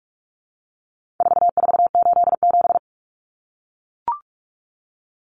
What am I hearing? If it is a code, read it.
Morse code, heard as a keyed tone, about 44 wpm, 700 Hz. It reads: 4487
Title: Numbers - 4-Digit 45wpm